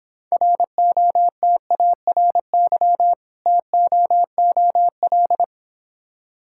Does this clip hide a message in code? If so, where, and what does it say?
Yes: Morse code from 0.3 to 5.4 s, heard as a keyed tone, about 26 wpm, 700 Hz. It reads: ROTARY TOOL